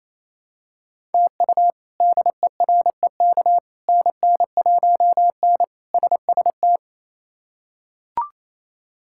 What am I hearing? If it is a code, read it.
Morse code, heard as a keyed tone, about 28 wpm, 700 Hz. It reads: TU DEREK NN1N SST
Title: K1USN SST QSO Practice - 28wpm